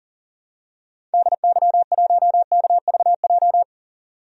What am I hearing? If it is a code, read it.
Morse code, heard as a keyed tone, about 40 wpm, 700 Hz. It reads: DY1KVJ
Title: DXCC-100 Call Signs - 6 Characters - 40wpm